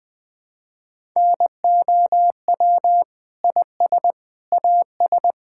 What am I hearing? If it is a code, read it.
Morse code, heard as a keyed tone, about 20 wpm, 700 Hz. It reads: NOW IS AS
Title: Sets of 8 Words - 20wpm